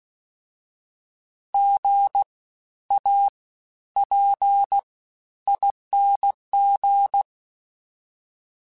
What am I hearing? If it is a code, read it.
Morse code, heard as a keyed tone, about 16 wpm, spaced as for 11 wpm, 800 Hz. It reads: GAPING
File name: Special INGs-mp3s / Special INGs-mp3s / SlowToFaster ing @15